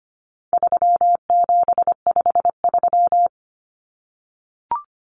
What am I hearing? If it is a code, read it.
Morse code, heard as a keyed tone, about 25 wpm, 700 Hz. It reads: 3753
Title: Common Cut Numbers - 4-Digit 25wpm